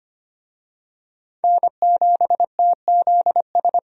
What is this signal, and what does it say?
Morse code, heard as a keyed tone, about 25 wpm, 700 Hz. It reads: N7TZS
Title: USA Call Signs - 5 Characters - 25wpm